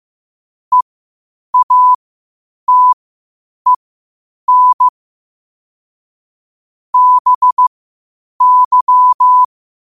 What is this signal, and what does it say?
Morse code, heard as a keyed tone, about 15 wpm, spaced as for 8 wpm, 1000 Hz. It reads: EATEN BY
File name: qotdfiles / 8wpm